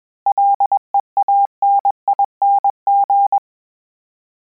Morse code, heard as a keyed tone, about 21 wpm, 800 Hz. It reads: LEANING
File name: Combinations / Combinations-mp3s / Combinations@20